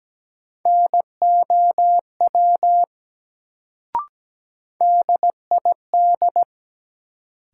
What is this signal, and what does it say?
Morse code, heard as a keyed tone, about 17 wpm, 700 Hz. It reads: NOW DID
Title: Sentences from Top 100 Words - New Word Drill - 17wpm